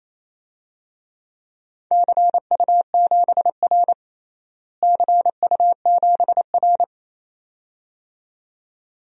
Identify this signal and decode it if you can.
Morse code, heard as a keyed tone, about 27 wpm, 700 Hz. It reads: CU7R CU7R
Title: DXCC-100 Call Signs - 4 Characters - Repeated 2x - 28wpm